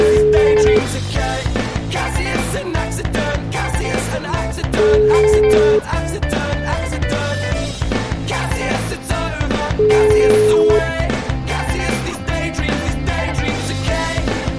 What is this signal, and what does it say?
Ringback tone (Europe, 425 Hz), with noise.